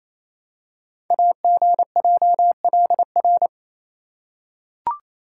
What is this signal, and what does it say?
Morse code, heard as a keyed tone, about 27 wpm, 700 Hz. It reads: AGJLR
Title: Code Groups - 5 Letters - 28wpm